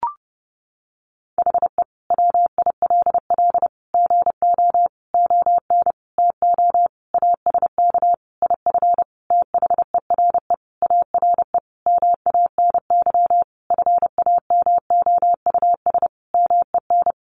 Morse code, heard as a keyed tone, about 30 wpm, 700 Hz. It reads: HE WILL GO ON TO ASK IF THERE ARE MANY FAMOUS MEN
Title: Sentences from Top 1500 Words 30wpm